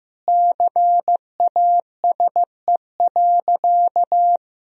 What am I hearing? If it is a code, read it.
Morse code, heard as a keyed tone, about 16 wpm, 700 Hz. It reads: CASE.
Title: Sentences from Top 1000 Words - Review Part 3 - 15wpm